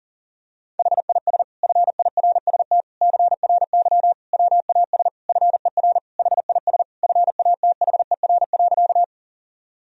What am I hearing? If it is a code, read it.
Morse code, heard as a keyed tone, about 37 wpm, 700 Hz. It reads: HIS FIRST CRY WAS FER HIS FATHER.
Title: Sentences from Top 500 Words and Abbreviations 40wpm